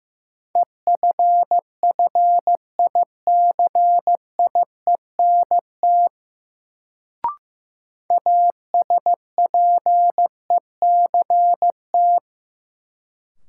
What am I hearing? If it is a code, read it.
Morse code, heard as a keyed tone, about 15 wpm, 700 Hz. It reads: EFFICIENT ASPECT
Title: Sentences from Top 1300 Words - New Word Drill - 15wpm